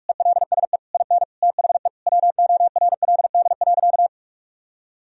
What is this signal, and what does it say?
Morse code, heard as a keyed tone, about 46 wpm, 700 Hz. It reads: ELSE IN THE WORLD.
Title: Sentences from Top 1000 Words - Review Part 4 - 45wpm